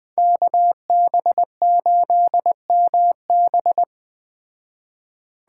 Morse code, heard as a keyed tone, about 21 wpm, 700 Hz. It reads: KB8MB
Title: Speed Racing - USA 2x2 Call Signs 17wpm